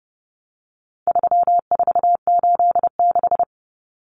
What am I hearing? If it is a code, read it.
Morse code, heard as a keyed tone, about 30 wpm, 700 Hz. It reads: 3486
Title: Common Cut Numbers - 4-Digit 30wpm